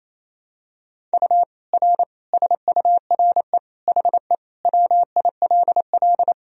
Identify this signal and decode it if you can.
Morse code, heard as a keyed tone, about 28 wpm, 700 Hz. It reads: U R SURE HE WILL
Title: Sentences from Top 300 Words and Abbreviations 28wpm